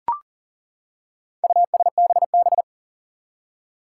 Morse code, heard as a keyed tone, about 40 wpm, 700 Hz. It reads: USBB
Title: Four Challenging Letter-Number Combinations 40wpm